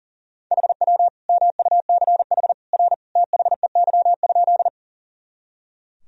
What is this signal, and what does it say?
Morse code, heard as a keyed tone, about 40 wpm, 700 Hz. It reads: HW MUCH R THEY?
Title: Sentences from Top 200 Words and Abbreviations 40wpm